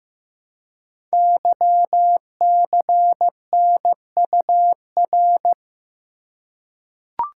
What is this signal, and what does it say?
Morse code, heard as a keyed tone, about 15 wpm, 700 Hz. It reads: YCNUR
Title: Code Groups - 5 Letter:Numbers - 15wpm